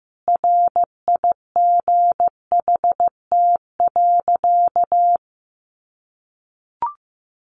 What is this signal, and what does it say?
Morse code, heard as a keyed tone, about 15 wpm, 700 Hz. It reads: RIGHT.
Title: Sentences from Top 300 Words 15wpm